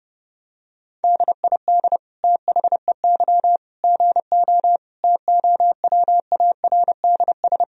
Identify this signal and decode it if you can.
Morse code, heard as a keyed tone, about 32 wpm, 700 Hz. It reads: DID THEY GO TOWARDS
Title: Sentences from Top 1000 Words - Review Part 2 - 30wpm